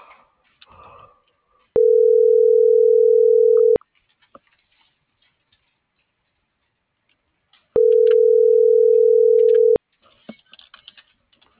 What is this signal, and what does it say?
Ringback tone (North America), 440 + 480 Hz.
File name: CENTRAL SOTOVA / out alejandro.rodriguez - .49597